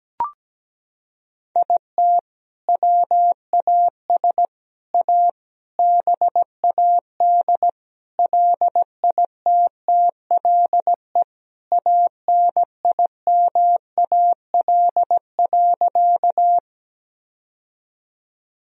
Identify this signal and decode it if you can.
Morse code, heard as a keyed tone, about 17 wpm, 700 Hz. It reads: IT WAS A BAD LITTLE ANIMAL.
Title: Sentences from Top 1000 Words - Review Part 1 - 17wpm